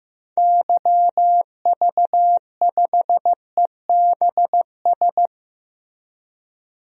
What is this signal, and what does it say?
Morse code, heard as a keyed tone, about 15 wpm, 700 Hz. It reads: YV5EBS
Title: DXCC-100 Call Signs - 6 Characters - Repeated 2x - 15wpm